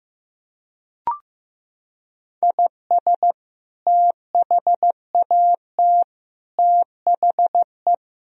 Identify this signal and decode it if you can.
Morse code, heard as a keyed tone, about 16 wpm, 700 Hz. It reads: IS THAT THE
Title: Sentences from Top 1000 Words - Review Part 2 - 15wpm